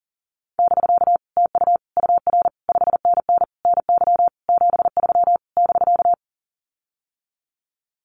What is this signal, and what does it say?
Morse code, heard as a keyed tone, about 40 wpm, 700 Hz. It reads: <BK> TU UR 5NN NY 73 <BK>
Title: Common QSO Phrases Encore 40wpm